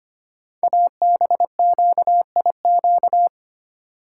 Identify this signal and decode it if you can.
Morse code, heard as a keyed tone, about 25 wpm, 700 Hz. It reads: ABQIQ
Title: Code Groups - 5 Letters - 25wpm